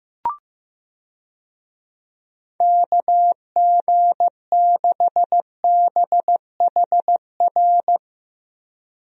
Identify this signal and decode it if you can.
Morse code, heard as a keyed tone, about 15 wpm, 700 Hz. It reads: KG6BHR
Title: USA Call Signs - 6 Characters - Repeated 3x - 15wpm